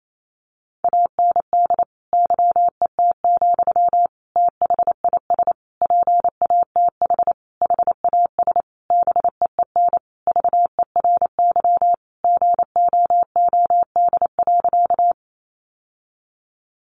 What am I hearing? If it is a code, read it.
Morse code, heard as a keyed tone, about 28 wpm, 700 Hz. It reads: AND YET, THIS PATH HAS BEEN VERY GOOD.
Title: Sentences from Top 1400 Words 28wpm